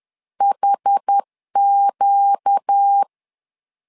Morse code, heard as a keyed tone, about 10 wpm, 800 Hz. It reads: HQ